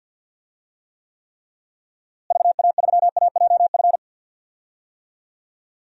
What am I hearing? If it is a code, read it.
Morse code, heard as a keyed tone, about 48 wpm, 700 Hz. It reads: VA3AJF
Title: DXCC-100 Call Signs - 6 Characters - Repeated 3x - 50wpm